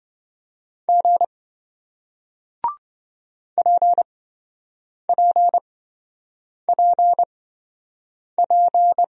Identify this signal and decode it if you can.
Morse code, heard as a keyed tone, about 27 wpm, spaced as for 7 wpm, 700 Hz. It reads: G PPPP